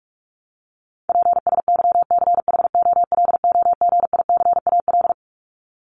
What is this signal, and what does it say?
Morse code, heard as a keyed tone, about 44 wpm, 700 Hz. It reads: PSYCHOLOGICAL
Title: Top 1251-1500 Words 45wpm